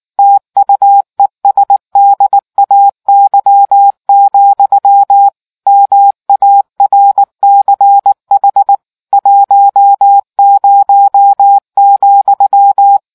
Morse code, heard as a keyed tone, about 18 wpm, 800 Hz. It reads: TUESDAY, MARCH 10,